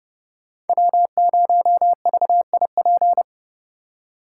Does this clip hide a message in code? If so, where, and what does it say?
Yes: Morse code from 0.7 to 3.2 s, heard as a keyed tone, about 29 wpm, 700 Hz. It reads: W0VIP